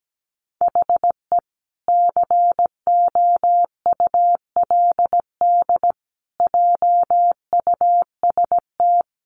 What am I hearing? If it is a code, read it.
Morse code, heard as a keyed tone, about 17 wpm, 700 Hz. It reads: HE COULD JUST
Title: Sentences from Top 1100 Words 17wpm